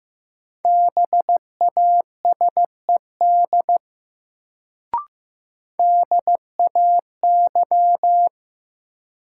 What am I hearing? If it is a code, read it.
Morse code, heard as a keyed tone, about 15 wpm, 700 Hz. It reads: BASED DAY
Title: Sets of 2 Words - New Word Drill - 15wpm